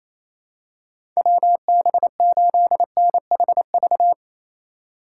Morse code, heard as a keyed tone, about 28 wpm, 700 Hz. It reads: WB8NHV
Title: USA Call Signs - 3x Character Spacing - 28wpm